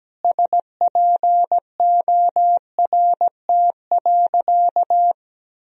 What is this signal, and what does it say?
Morse code, heard as a keyed tone, about 17 wpm, 700 Hz. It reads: SPORT.
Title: Sentences from Top 1000 Words - Review Part 4 - 17wpm